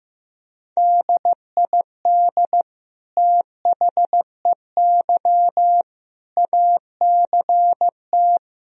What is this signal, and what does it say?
Morse code, heard as a keyed tone, about 16 wpm, 700 Hz. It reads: DID THEY ACT
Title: Sentences from Top 1000 Words - Review Part 4 - 15wpm